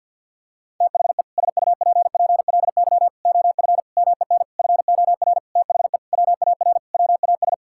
Morse code, heard as a keyed tone, about 52 wpm, 700 Hz. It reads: THE SUPPLY OF MEN FOR THE WAR WAS
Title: Sentences from Top 1000 Words - Review Part 6 - 50wpm